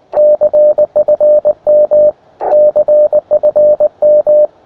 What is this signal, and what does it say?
Morse code, heard as a keyed tone, about 19 wpm, 600 Hz. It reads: CFM CFM
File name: attachments / uy1if 20m